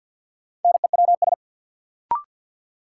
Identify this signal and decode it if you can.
Morse code, heard as a keyed tone, about 53 wpm, 700 Hz. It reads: NEWS
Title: Sets of 5 Words - New Word Drill - 50wpm